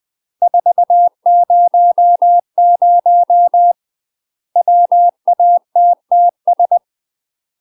Morse code, heard as a keyed tone, about 20 wpm, 700 Hz. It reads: 400 WATTS
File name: UK MASTER QSOs TRAINING / 20WPM / UK MASTER QSOs 0008